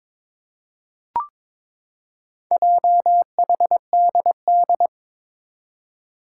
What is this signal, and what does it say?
Morse code, heard as a keyed tone, about 22 wpm, 700 Hz. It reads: JHDD
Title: Four Challenging Letter-Number Combinations 22wpm